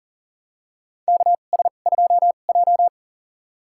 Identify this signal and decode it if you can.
Morse code, heard as a keyed tone, about 40 wpm, spaced as for 29 wpm, 700 Hz. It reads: KS2J
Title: USA Call Signs - 4 Characters - 2x Character Spacing - 40wpm